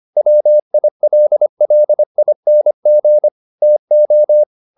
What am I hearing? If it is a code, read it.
Morse code, heard as a keyed tone, about 25 wpm, 600 Hz. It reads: WILLING TO